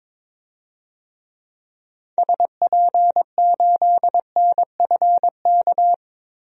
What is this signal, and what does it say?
Morse code, heard as a keyed tone, about 22 wpm, 700 Hz. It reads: SP8NFK
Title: DXCC-100 Call Signs - 6 Characters - 22wpm